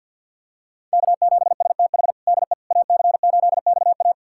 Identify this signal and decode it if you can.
Morse code, heard as a keyed tone, about 51 wpm, 700 Hz. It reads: K7STH DE AC8XA
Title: USA Callsign DE Callsign - 5 Characters - 50wpm